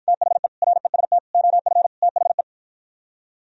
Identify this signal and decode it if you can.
Morse code, heard as a keyed tone, about 52 wpm, 700 Hz. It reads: THE REST OF THE